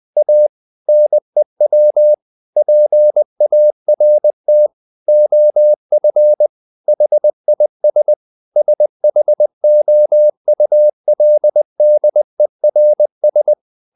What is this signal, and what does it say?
Morse code, heard as a keyed tone, about 20 wpm, 600 Hz. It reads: A NEW PART OF HIS SHOULDERS
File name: the old man and the sea / the old man and the sea part 1 20wpm